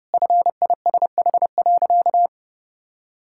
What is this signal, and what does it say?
Morse code, heard as a keyed tone, about 32 wpm, 700 Hz. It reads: FISH.
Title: Sentences from Top 1600 Words 30wpm